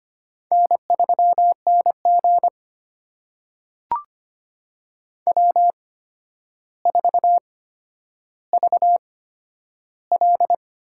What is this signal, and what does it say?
Morse code, heard as a keyed tone, about 25 wpm, spaced as for 11 wpm, 700 Hz. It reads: N3NG W4VL
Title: USA Call Signs - 4 Characters - 8x Character Spacing - 25wpm